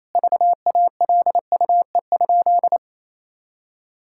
Morse code, heard as a keyed tone, about 28 wpm, 700 Hz. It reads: VALUE?
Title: Sentences from Top 1000 Words - Review Part 6 - 28wpm